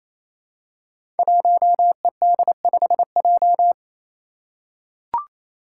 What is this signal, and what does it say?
Morse code, heard as a keyed tone, about 28 wpm, 700 Hz. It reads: 1ED5J
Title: Five Challenging Letter-Number Combinations 28wpm